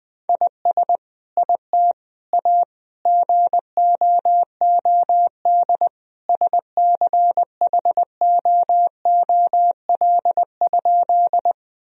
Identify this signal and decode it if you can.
Morse code, heard as a keyed tone, about 20 wpm, 700 Hz. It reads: IS IT A GOOD SCHOOL?
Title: Sentences from Top 1000 Words - Review Part 1 - 20wpm